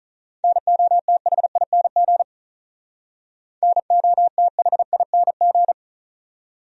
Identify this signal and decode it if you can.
Morse code, heard as a keyed tone, about 46 wpm, 700 Hz. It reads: NOTHING NOTHING